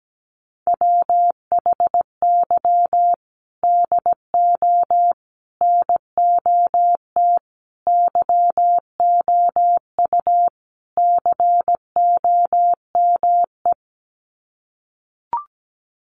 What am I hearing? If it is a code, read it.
Morse code, heard as a keyed tone, about 17 wpm, 700 Hz. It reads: WHY DO NOT YOU COME
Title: Sets of 5 Words Encore2 17wpm